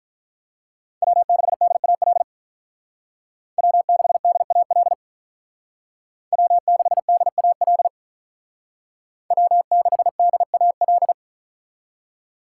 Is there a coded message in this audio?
Yes: Morse code from 1.0 to 11.1 s, heard as a keyed tone, about 45 wpm, 700 Hz. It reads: W6DAL W6DAL W6DAL W6DAL